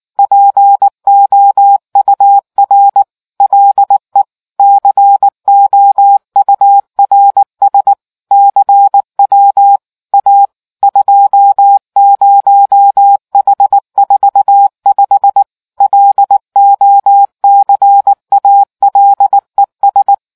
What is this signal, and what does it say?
Morse code, heard as a keyed tone, about 19 wpm, 800 Hz. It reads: POUR LE COURS CW A 20H45 LOCALES